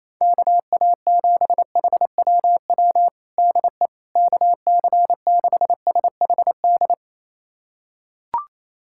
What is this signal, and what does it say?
Morse code, heard as a keyed tone, about 27 wpm, 700 Hz. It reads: KA7HWW DE KC6SHD
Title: USA Callsign DE Callsign 28wpm